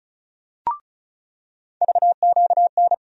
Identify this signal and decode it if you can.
Morse code, heard as a keyed tone, about 36 wpm, 700 Hz. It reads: VQN
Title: USA Call Sign Postfixes - 3 Letters 35wpm